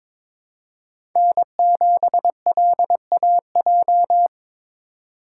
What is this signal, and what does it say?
Morse code, heard as a keyed tone, about 21 wpm, 700 Hz. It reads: N7LAJ
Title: USA Call Signs - 5 Characters - 4x Character Spacing - 22wpm